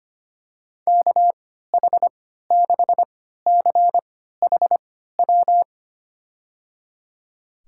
Morse code, heard as a keyed tone, about 25 wpm, spaced as for 14 wpm, 700 Hz. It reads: KH6CHW
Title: USA Call Signs - 6 Characters - 3x Character Spacing - 25wpm